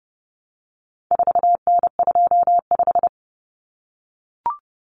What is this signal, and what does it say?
Morse code, heard as a keyed tone, about 30 wpm, 700 Hz. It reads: 4N25
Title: Common Cut Numbers - 4-Digit 30wpm